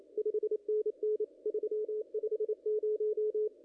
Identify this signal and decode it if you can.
Morse code, heard as a keyed tone, about 28 wpm, 410 Hz. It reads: 5NN 350
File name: contests / ss-cw-k5tr-op-n5rz -1800